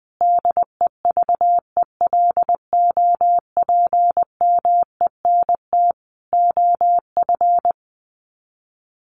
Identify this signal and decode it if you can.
Morse code, heard as a keyed tone, about 20 wpm, 700 Hz. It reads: DEVELOPMENT OF
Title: Sets of 3 Words 20wpm